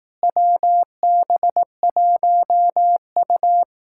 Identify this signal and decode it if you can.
Morse code, heard as a keyed tone, about 18 wpm, 700 Hz. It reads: WB1U
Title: Speed-Racing Common QSO Phrases Encore - 15wpm